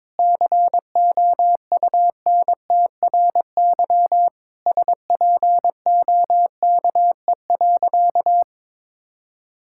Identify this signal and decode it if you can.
Morse code, heard as a keyed tone, about 22 wpm, 700 Hz. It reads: COUNTRY SPOKE.
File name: practice / 22wpm / Sentences with 3-Word Phrases - 22wpm